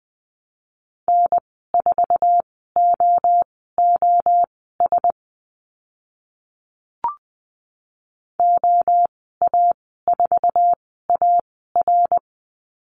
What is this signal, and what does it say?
Morse code, heard as a keyed tone, about 20 wpm, spaced as for 14 wpm, 700 Hz. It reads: N4OOS OA4AR
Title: 062 Lesson 11g - Letter h - Call Signs 20wpm